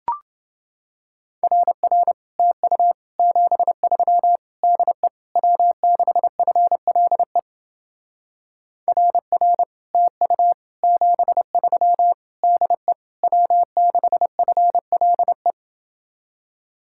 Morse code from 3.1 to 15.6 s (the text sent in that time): 73 DE W6FLE RR TU 73 DE W6FLE